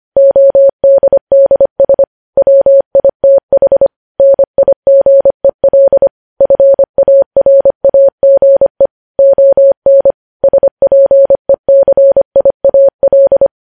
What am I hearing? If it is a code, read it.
Morse code, heard as a keyed tone, about 25 wpm, 550 Hz. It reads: ODDS WITH NIGEL FARAGE ON SPECIAL
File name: podcast / News headlines-1x-25x25